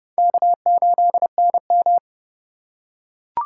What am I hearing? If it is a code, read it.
Morse code, heard as a keyed tone, about 28 wpm, 700 Hz. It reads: K8NM